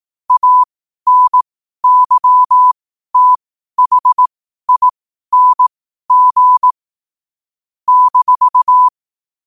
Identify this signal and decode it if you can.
Morse code, heard as a keyed tone, about 18 wpm, spaced as for 12 wpm, 1000 Hz. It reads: ANYTHING -
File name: qotdfiles / 12wpm